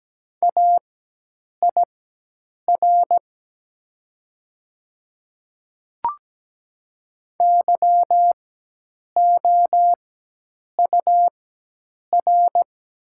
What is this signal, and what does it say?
Morse code, heard as a keyed tone, about 17 wpm, spaced as for 8 wpm, 700 Hz. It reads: AIR YOUR
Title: 4x Character Spacing - Top 200 Words - 17wpm